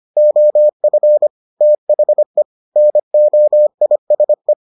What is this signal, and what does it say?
Morse code, heard as a keyed tone, about 25 wpm, 600 Hz. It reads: OF THE NOISE